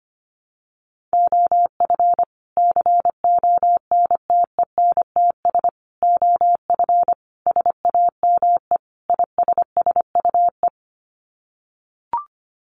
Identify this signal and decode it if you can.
Morse code, heard as a keyed tone, about 25 wpm, 700 Hz. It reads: OF CONTENTS OF SAME ISSUE
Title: Sets of 5 Words Encore2 25wpm